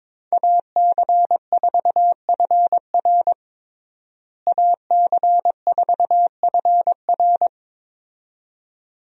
Morse code, heard as a keyed tone, about 21 wpm, 700 Hz. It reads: AC4FR AC4FR
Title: USA Call Signs - 5 Characters - Repeated 2x - 22wpm